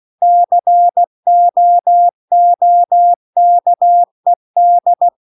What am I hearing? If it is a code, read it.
Morse code, heard as a keyed tone, about 16 wpm, 700 Hz. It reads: COOKED